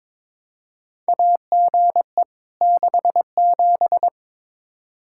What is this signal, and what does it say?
Morse code, heard as a keyed tone, about 22 wpm, 700 Hz. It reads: AGE 67
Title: K1USN SST QSO Practice 22wpm